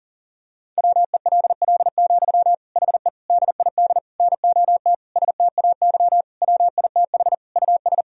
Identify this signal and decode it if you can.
Morse code, heard as a keyed tone, about 37 wpm, 700 Hz. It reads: WELL, HE DID NOT STAY WITH US
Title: Sentences from Top 1000 Words - Review Part 2 - 40wpm